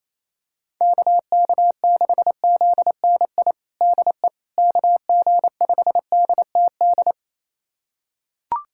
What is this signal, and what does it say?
Morse code, heard as a keyed tone, about 28 wpm, 700 Hz. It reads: KK6ZNI DE KG5DTD
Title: USA Callsign DE Callsign - 6 Characters - 28wpm